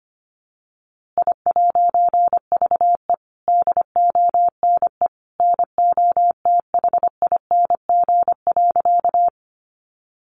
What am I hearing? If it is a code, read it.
Morse code, heard as a keyed tone, about 25 wpm, 700 Hz. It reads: I'VE DONE NOTHING.
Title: A Sound Of Thunder 25wpm